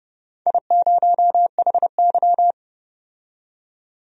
Morse code, heard as a keyed tone, about 30 wpm, 700 Hz. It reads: I0HY
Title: DXCC-100 Call Signs - 4 Characters - 30wpm